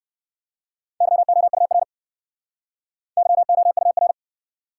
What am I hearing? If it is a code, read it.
Morse code, heard as a keyed tone, about 58 wpm, 700 Hz. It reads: YOUR YOUR